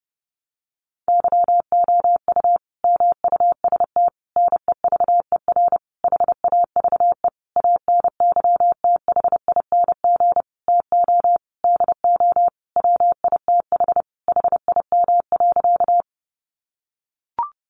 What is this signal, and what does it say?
Morse code, heard as a keyed tone, about 28 wpm, 700 Hz. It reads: YOU MUST NEVER HAVE ANYTHING TO DO WITH HIM.
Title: Sentences from Top 1000 Words 30wpm